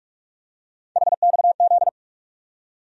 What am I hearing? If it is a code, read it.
Morse code, heard as a keyed tone, about 47 wpm, 700 Hz. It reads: HXZ
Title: USA Call Sign Postfixes - 3 Letters 45wpm